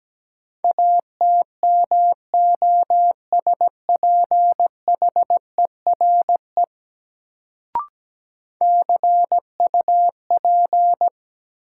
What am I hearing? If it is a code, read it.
Morse code, heard as a keyed tone, about 17 wpm, 700 Hz. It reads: ATMOSPHERE CUP
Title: Sentences from Top 1600 Words - New Word Drill Alt - 17wpm